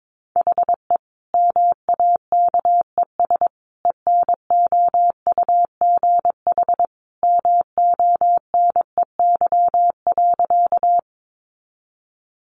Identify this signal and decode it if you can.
Morse code, heard as a keyed tone, about 22 wpm, 700 Hz. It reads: HE MAKES ENOUGH MONEY.
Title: Additional Easy Phrases 22wpm